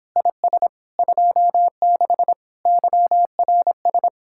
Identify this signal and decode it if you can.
Morse code, heard as a keyed tone, about 31 wpm, 700 Hz. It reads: IS 26 YRS